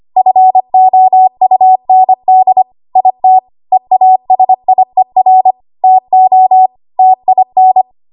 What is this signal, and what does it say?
Morse code, heard as a keyed tone, about 25 wpm, 750 Hz. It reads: FOUND IT EASIER TO TIN
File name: Archive / 25 WPM / 25WPM